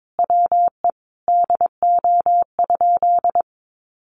Morse code, heard as a keyed tone, about 22 wpm, 700 Hz. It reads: WE DO?
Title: Sentences from Top 1300 Words 22wpm